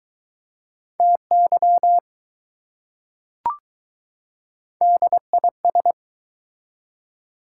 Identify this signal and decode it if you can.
Morse code, heard as a keyed tone, about 20 wpm, 700 Hz. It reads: TY DIS